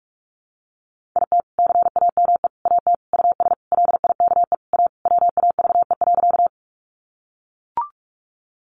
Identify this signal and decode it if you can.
Morse code, heard as a keyed tone, about 44 wpm, 700 Hz. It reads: IT CAME AT US LIKE A WAVE.
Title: Sentences from Top 1400 Words 45wpm